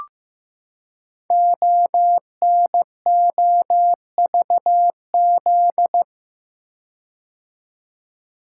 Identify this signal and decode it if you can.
Morse code, heard as a keyed tone, about 15 wpm, 700 Hz. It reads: ONOVZ
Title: Code Groups - 5 Letters - 15wpm